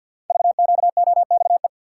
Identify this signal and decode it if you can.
Morse code, heard as a keyed tone, about 48 wpm, 700 Hz. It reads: VQOXE
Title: Code Groups - 5 Letters - 50wpm